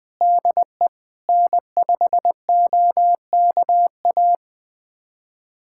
Morse code, heard as a keyed tone, about 20 wpm, 700 Hz. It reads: DE N5OKA
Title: USA Callsign DE Callsign - 5 Characters - 20wpm